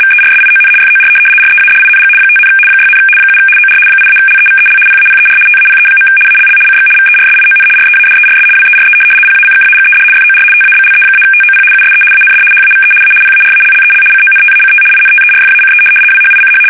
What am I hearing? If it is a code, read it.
RTTY, about 75 baud. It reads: VERSION FOR READING 5, 6, 7 OR 8-LEVEL PAPER TAPE. ALL CX VERSION READERS WERE PACKAGED FOR TABLE-TOP USE AND HAVE A PARALLEL CONTACT INTERFACE WHICH
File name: fsk850-75B / teletype-corp